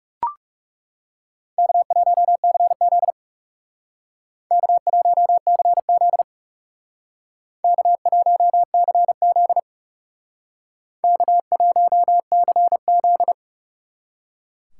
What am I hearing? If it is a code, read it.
Morse code, heard as a keyed tone, about 39 wpm, 700 Hz. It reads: K1CZ K1CZ K1CZ K1CZ